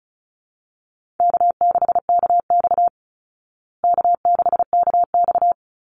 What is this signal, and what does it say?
Morse code, heard as a keyed tone, about 34 wpm, 700 Hz. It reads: K6KX K6KX
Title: Easy Call Signs Repeated 3x - 35wpm